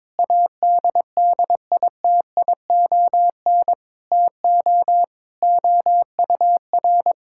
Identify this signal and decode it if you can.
Morse code, heard as a keyed tone, about 22 wpm, 700 Hz. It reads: ADDITION TO OUR
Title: Sentences from Top 1000 Words - Review Part 6 - 22wpm